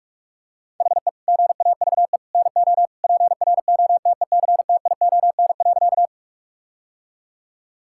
Morse code, heard as a keyed tone, about 46 wpm, 700 Hz. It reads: HE GAVE NO PROTECTION.
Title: Sentences from Top 1600 Words 45wpm